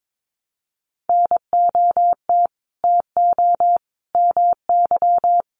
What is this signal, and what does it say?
Morse code, heard as a keyed tone, about 22 wpm, 700 Hz. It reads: NOT TO MY
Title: Sentences from Top 600 Words 22wpm